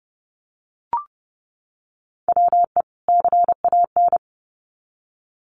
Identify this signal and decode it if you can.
Morse code, heard as a keyed tone, about 30 wpm, 700 Hz. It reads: WE CAN
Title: Sets of 2 Words 30wpm